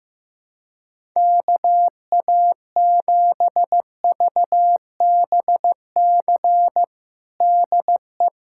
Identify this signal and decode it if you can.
Morse code, heard as a keyed tone, about 15 wpm, 700 Hz. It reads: KA7VBC DE
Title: USA Callsign DE Callsign - 6 Characters - 15wpm